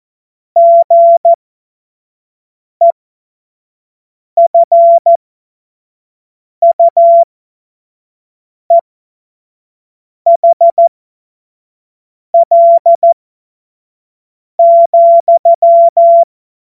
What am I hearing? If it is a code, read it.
Morse code, heard as a keyed tone, about 14 wpm, spaced as for 5 wpm, 680 Hz. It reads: GEFUEHL,
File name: Morsekurs / Witze / Witz 26 März 2024 25 F70